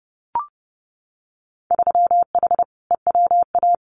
Morse code, heard as a keyed tone, about 30 wpm, 700 Hz. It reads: 3H EWA
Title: Winter Field Day Exchanges Encore 30wpm